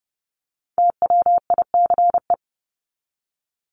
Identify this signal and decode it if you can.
Morse code, heard as a keyed tone, about 30 wpm, 700 Hz. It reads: TWICE
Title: Top 1501-1750 Words 30wpm